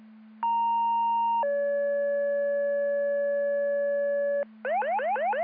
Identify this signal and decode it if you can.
Two-tone page, heard as a one-second tone, then a three-second tone, about 930 then 570 Hz, with noise. It is heard from 0.4 to 4.4 s.